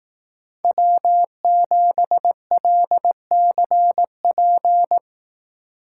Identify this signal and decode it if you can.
Morse code, heard as a keyed tone, about 18 wpm, 700 Hz. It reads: W7LCP
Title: Speed Racing - USA 1x3 Call Signs 15wpm